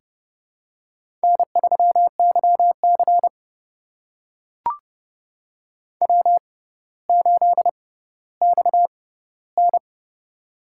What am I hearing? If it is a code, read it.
Morse code, heard as a keyed tone, about 31 wpm, spaced as for 15 wpm, 700 Hz. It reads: N3YC W8XN
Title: USA Call Signs - 4 Characters - 6x Character Spacing - 30wpm